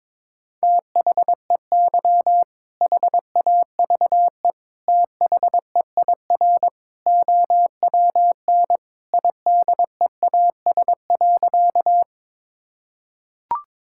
Morse code, heard as a keyed tone, about 22 wpm, 700 Hz. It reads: THEY HAVE THEIR OWN IDEAS.
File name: practice / 22wpm / Sentences with 2-Word Phrases - 22wpm